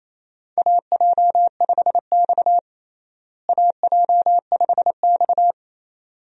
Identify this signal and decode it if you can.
Morse code, heard as a keyed tone, about 28 wpm, 700 Hz. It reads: AJ5X AJ5X
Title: USA Call Signs - 4 Characters - Repeated 2x - 28wpm